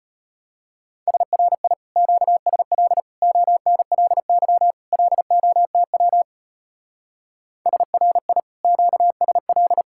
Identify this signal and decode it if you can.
Morse code, heard as a keyed tone, about 30 wpm, 700 Hz. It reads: SRI QSL ONLY LOTW SRI QSL
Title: Speed-Racing Common QSO Phrases - 25wpm